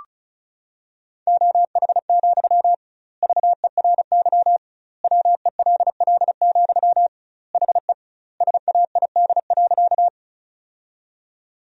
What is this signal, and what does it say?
Morse code, heard as a keyed tone, about 34 wpm, 700 Hz. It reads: OH, VERY WELL, HE SAID.
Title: 2x Word-Spacing - Sentences from Top 500 Words - 35wpm